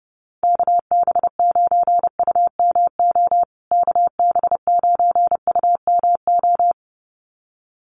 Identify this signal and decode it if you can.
Morse code, heard as a keyed tone, about 30 wpm, 700 Hz. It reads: KB9UMO KB9UMO
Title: USA Call Signs Repeated 2x 30wpm